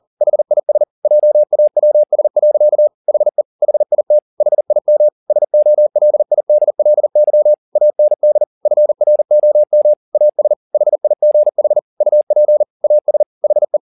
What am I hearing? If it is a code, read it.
Morse code, heard as a keyed tone, about 40 wpm, 600 Hz. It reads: HIS JAWS. HE HIT HIM SOLIDLY AND FROM AS HIGH UP AS HE
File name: the old man and the sea 2 40wpm